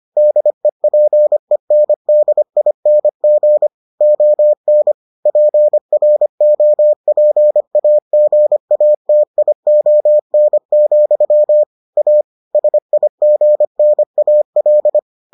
Morse code, heard as a keyed tone, about 25 wpm, 600 Hz. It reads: DEPENDING ON PROPAGATION, A SIGNAL